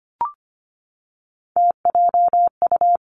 Morse code, heard as a keyed tone, about 25 wpm, 700 Hz. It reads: TJU
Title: Three Challenging Letter-Number Combinations 25wpm